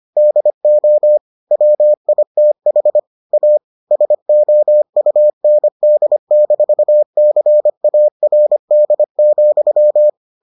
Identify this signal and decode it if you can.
Morse code, heard as a keyed tone, about 25 wpm, 600 Hz. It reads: DO WITH A SOUND-CARD,